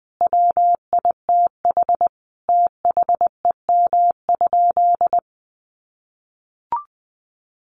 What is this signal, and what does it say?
Morse code, heard as a keyed tone, about 19 wpm, 700 Hz. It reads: WITH THEM?
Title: Sentences from Top 600 Words 20wpm